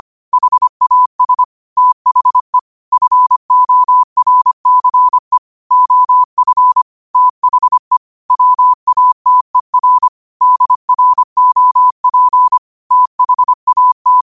Morse code, heard as a keyed tone, about 25 wpm, 1000 Hz. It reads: HAS THE FORCE OF THE WATER DROP THAT
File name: qotdfiles / 25wpm